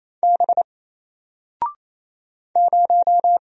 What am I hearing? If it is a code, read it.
Morse code, heard as a keyed tone, about 28 wpm, 700 Hz. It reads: B 0
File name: Single-Letter-Number-28wpm